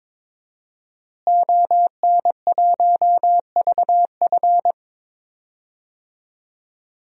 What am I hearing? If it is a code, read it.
Morse code, heard as a keyed tone, about 22 wpm, 700 Hz. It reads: ON1VF
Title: DXCC-100 Call Signs - 5 Characters - Repeated 3x - 22wpm